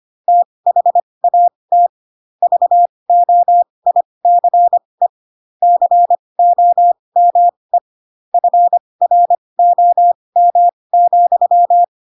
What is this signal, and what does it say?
Morse code, heard as a keyed tone, about 25 wpm, spaced as for 20 wpm, 700 Hz. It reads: THAT VOICE COME FROM,
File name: Pinocchio / Pinocchio ch1 15 25 ik1hge